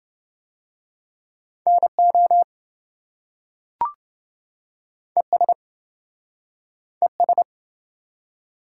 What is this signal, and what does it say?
Morse code, heard as a keyed tone, about 27 wpm, 700 Hz. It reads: NO ES ES